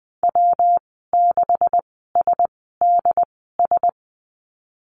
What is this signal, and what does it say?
Morse code, heard as a keyed tone, about 20 wpm, spaced as for 15 wpm, 700 Hz. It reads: W6SDS
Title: 186 Lesson 30g - Letter b - Call Signs 20wpm